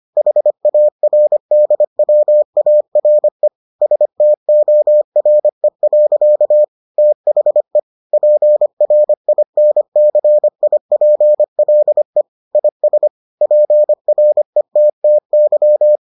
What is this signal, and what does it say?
Morse code, heard as a keyed tone, about 25 wpm, 600 Hz. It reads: HARDWARE STORE. THE PRINCIPLE IS PRETTY